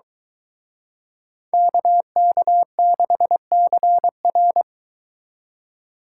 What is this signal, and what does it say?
Morse code, heard as a keyed tone, about 23 wpm, 700 Hz. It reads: KK6CR